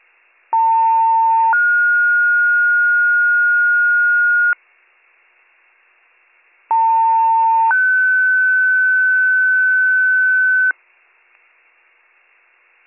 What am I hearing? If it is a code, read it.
Two-tone page, heard as a one-second tone, then a three-second tone, about 900 then 1430 Hz, about 900 then 1500 Hz. It is heard from 0.5 to 10.7 s.